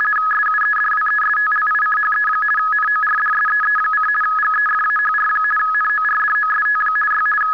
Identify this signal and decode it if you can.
RTTY, about 50 baud. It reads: SUEDWESTLICHE NORDSEE: GEFAHR NORDOST Y
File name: article211 / rtty demo